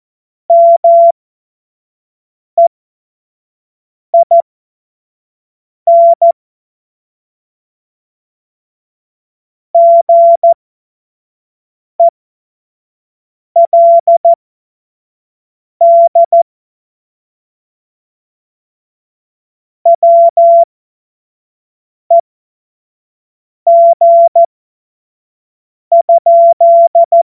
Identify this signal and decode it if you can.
Morse code, heard as a keyed tone, about 14 wpm, spaced as for 5 wpm, 680 Hz. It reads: MEIN GELD WEG?
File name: Morsekurs / Witze / Witz 27. Juni 2023 25 F70